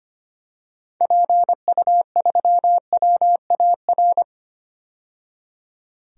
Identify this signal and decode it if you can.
Morse code, heard as a keyed tone, about 25 wpm, 690 Hz. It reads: PU3WAR
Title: DXCC-100 Call Signs - 6 Characters - 25wpm